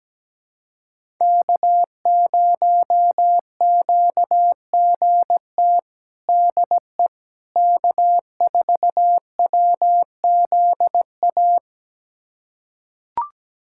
Morse code, heard as a keyed tone, about 17 wpm, 700 Hz. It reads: K0QGT DE K4WZA
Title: USA Callsign DE Callsign - 5 Characters - 17wpm